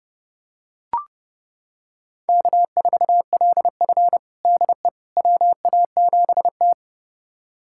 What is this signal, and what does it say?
Morse code, heard as a keyed tone, about 32 wpm, 700 Hz. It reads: K4LF DE WA7T
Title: USA Callsign DE Callsign - 4 Characters - 30wpm